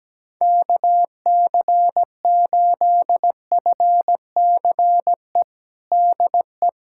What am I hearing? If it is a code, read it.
Morse code, heard as a keyed tone, about 17 wpm, 700 Hz. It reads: KC8FCE DE
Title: USA Callsign DE Callsign - 6 Characters - 17wpm